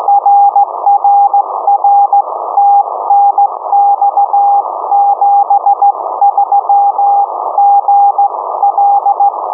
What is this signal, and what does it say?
Morse code, heard as a keyed tone, about 16 wpm, 810 Hz. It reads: RRR TNX 73 GL
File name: uploader / 50086 khz cw